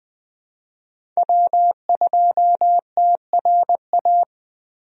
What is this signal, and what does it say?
Morse code, heard as a keyed tone, about 20 wpm, 700 Hz. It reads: W2TRA
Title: USA Call Signs - 5 Characters - Repeated 3x - 20wpm